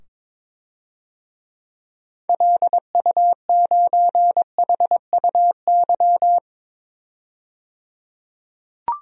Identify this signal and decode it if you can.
Morse code, heard as a keyed tone, about 22 wpm, 700 Hz. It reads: LU9HUY
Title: DXCC-100 Call Signs - 6 Characters - 22wpm